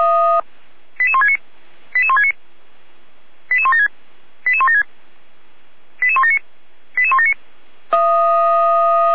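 Five-tone selective call (ZVEI), heard as quick runs of five tones, tones of about 70 ms. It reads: 88169 88169 88176 88176 88179 88179
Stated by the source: Five-tone ZVEI selective call received on an SDR.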